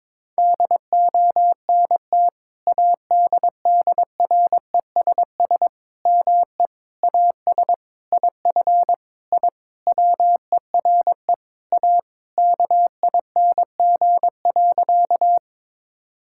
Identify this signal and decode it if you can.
Morse code, heard as a keyed tone, about 22 wpm, 700 Hz. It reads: DONT ADDRESS ME AS IF I WERE A KING.
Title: Sentences from Top 1000 Words - Review Part 6 - 22wpm